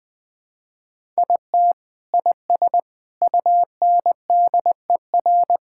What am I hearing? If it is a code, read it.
Morse code, heard as a keyed tone, about 21 wpm, 700 Hz. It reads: IT IS UNDER
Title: Sentences from Top 1000 Words - Review Part 6 - 20wpm